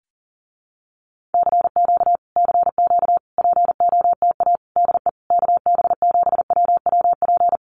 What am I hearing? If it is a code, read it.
Morse code, heard as a keyed tone, about 37 wpm, 700 Hz. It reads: CQ CQ POTA DE KB7WWP
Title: Common QSO Phrases Encore 40wpm